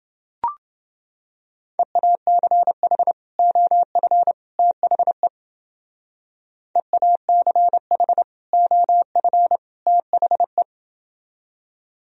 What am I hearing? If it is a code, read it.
Morse code, heard as a keyed tone, about 28 wpm, 700 Hz. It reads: EACH OF THE EACH OF THE